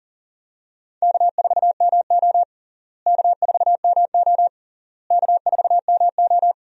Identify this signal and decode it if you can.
Morse code, heard as a keyed tone, about 40 wpm, 700 Hz. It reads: K4MO K4MO K4MO
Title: USA Call Signs - 4 Characters - Repeated 3x - 40wpm